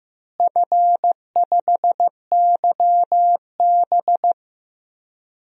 Morse code, heard as a keyed tone, about 15 wpm, 700 Hz. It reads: F5YB
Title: DXCC-100 Call Signs - 4 Characters - Repeated 3x - 15wpm